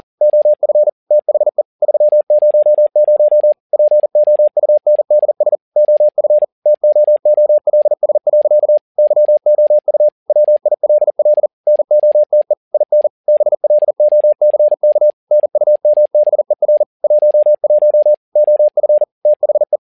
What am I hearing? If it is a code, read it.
Morse code, heard as a keyed tone, about 40 wpm, 600 Hz. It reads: OF THE 300 POUNDS OF TOOLS. YOU WILL NOTE IN BLOCK NUMBER 11 OF THE
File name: learning center / tower trouble / tower touble 40wpm